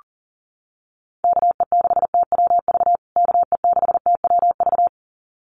Morse code, heard as a keyed tone, about 40 wpm, 700 Hz. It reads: KE6TWV KE6TWV
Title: USA Call Signs Repeated 2x 40wpm